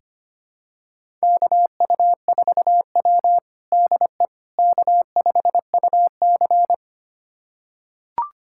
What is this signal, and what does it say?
Morse code, heard as a keyed tone, about 25 wpm, 700 Hz. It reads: KU4W DE K5UC
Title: USA Callsign DE Callsign - 4 Characters - 25wpm